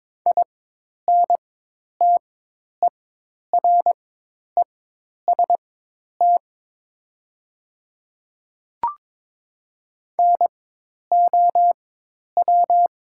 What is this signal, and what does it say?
Morse code, heard as a keyed tone, about 22 wpm, spaced as for 10 wpm, 700 Hz. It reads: INTEREST NOW
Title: 4x Character Spacing - Top 500 Words - 22wpm